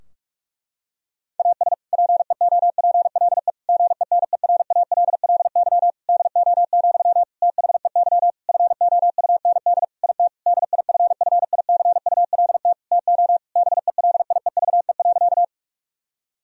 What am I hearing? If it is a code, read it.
Morse code, heard as a keyed tone, about 46 wpm, 700 Hz. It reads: AS PEOPLE GENERALLY DO, THEY FOUND IT DIFFICULT TO BELIEVE.
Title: Sentences from Top 1000 Words - Review Part 4 - 45wpm